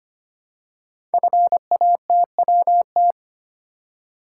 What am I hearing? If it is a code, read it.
Morse code, heard as a keyed tone, about 25 wpm, 700 Hz. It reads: FATWT
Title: Code Groups - 5 Letters - 25wpm